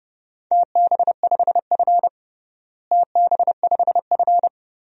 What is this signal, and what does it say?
Morse code, heard as a keyed tone, about 30 wpm, 700 Hz. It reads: TB5F TB5F
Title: DXCC-100 Call Signs - 4 Characters - Repeated 3x - 30wpm